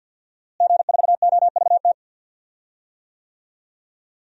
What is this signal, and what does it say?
Morse code, heard as a keyed tone, about 49 wpm, 700 Hz. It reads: G4OVT
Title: DXCC-100 Call Signs - 5 Characters - Repeated 3x - 50wpm